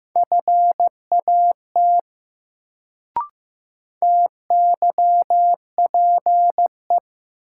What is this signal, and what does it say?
Morse code, heard as a keyed tone, about 15 wpm, 700 Hz. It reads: FAT TYPE
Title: Sentences from Top 600 Words - New Word Drill - 15wpm